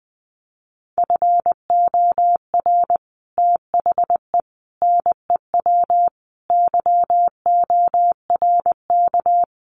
Morse code, heard as a keyed tone, about 20 wpm, 700 Hz. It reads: FOR THE NEW YORK
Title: Sets of 5 Words 20wpm